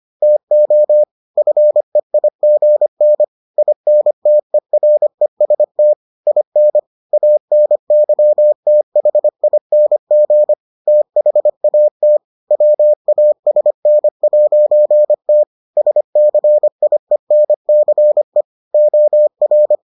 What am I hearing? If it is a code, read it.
Morse code, heard as a keyed tone, about 25 wpm, 600 Hz. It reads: TO FEIGN INTEREST IN ANYTHING THAT WASN'T SCIENCE OR